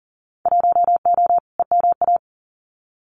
Morse code, heard as a keyed tone, about 40 wpm, 700 Hz. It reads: 1O EMA
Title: Winter Field Day Exchanges 40wpm